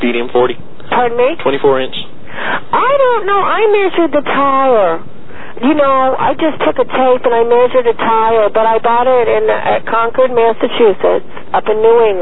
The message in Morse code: none